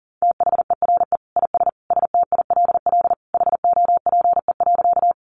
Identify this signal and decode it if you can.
Morse code, heard as a keyed tone, about 40 wpm, 700 Hz. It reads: THERE IS STILL HOPE.
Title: Sentences from Top 1000 Words 40wpm